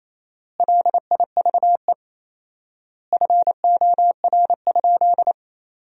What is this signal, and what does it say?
Morse code, heard as a keyed tone, about 28 wpm, 700 Hz. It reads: LIVE FOR?
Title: 4x Word-Spacing - Sentences from Top 200 Words - 28wpm